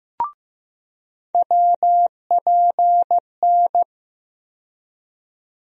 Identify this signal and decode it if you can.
Morse code, heard as a keyed tone, about 15 wpm, 700 Hz. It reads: WPN
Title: USA Call Sign Postfixes - 3 Letters 15wpm